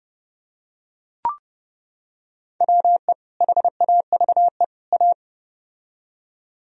Morse code, heard as a keyed tone, about 30 wpm, 700 Hz. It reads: WE HAVE A